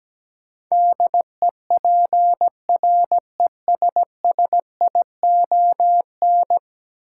Morse code, heard as a keyed tone, about 17 wpm, 700 Hz. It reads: DEPRESSION
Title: Sentences from Top 1300 Words - New Word Drill Alt - 17wpm